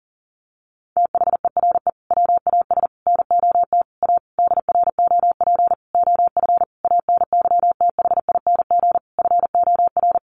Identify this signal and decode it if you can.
Morse code, heard as a keyed tone, about 37 wpm, 700 Hz. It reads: THERE WAS NOT A DROP OF ANYTHING FOR
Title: Sentences from Top 1500 Words 40wpm